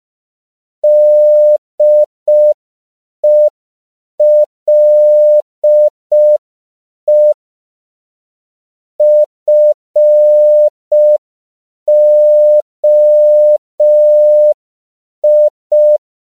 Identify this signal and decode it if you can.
Morse code, heard as a keyed tone, about 5 wpm, 600 Hz. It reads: DELE FOI